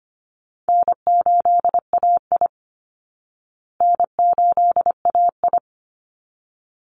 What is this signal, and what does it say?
Morse code, heard as a keyed tone, about 25 wpm, 700 Hz. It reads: N8AI N8AI
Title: Easy Call Signs Repeated 2x - 25wpm